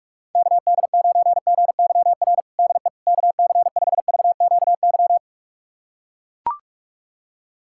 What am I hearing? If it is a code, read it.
Morse code, heard as a keyed tone, about 41 wpm, 700 Hz. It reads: KD9GYR DE KC5VQY
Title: USA Callsign DE Callsign - 6 Characters - 45wpm